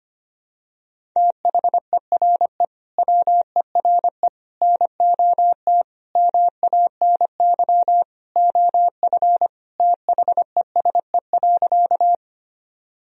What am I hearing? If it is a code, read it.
Morse code, heard as a keyed tone, about 25 wpm, 700 Hz. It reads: THERE WERE NOT MANY OF THESE.
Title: Sentences from Top 1000 Words - Review Part 1 - 25wpm